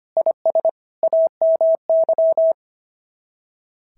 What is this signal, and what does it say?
Morse code, heard as a keyed tone, about 22 wpm, 650 Hz. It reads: IS AMY
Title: SOTA QSO Practice 25wpm